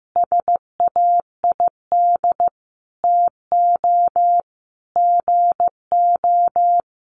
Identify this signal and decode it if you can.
Morse code, heard as a keyed tone, about 15 wpm, 700 Hz. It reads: SAID TO GO
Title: Sentences from Top 1300 Words 15wpm